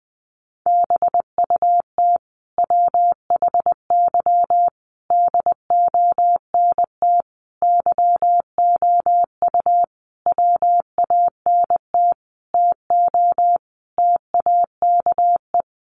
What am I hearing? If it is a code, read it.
Morse code, heard as a keyed tone, about 20 wpm, 700 Hz. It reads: BUT WHY DONT YOU WANT TO TAKE
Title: Sentences from Top 300 Words 20wpm